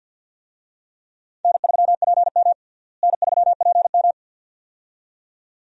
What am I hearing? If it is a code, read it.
Morse code, heard as a keyed tone, about 51 wpm, 700 Hz. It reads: N3PM N3PM
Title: USA Call Signs - 4 Characters - Repeated 2x - 50wpm